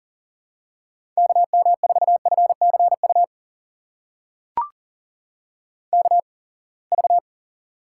Morse code, heard as a keyed tone, about 40 wpm, spaced as for 28 wpm, 700 Hz. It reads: KM4FCU KV
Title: USA Call Signs - 8x Character Spacing - 40wpm